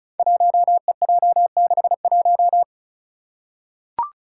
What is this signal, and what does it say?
Morse code, heard as a keyed tone, about 36 wpm, 700 Hz. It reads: 1EJ61
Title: Five Challenging Letter-Number Combinations 35wpm